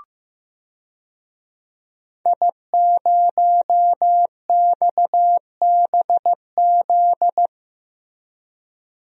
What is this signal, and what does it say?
Morse code, heard as a keyed tone, about 15 wpm, 700 Hz. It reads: I0XBZ
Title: DXCC-100 Call Signs - 5 Characters - Repeated 2x - 15wpm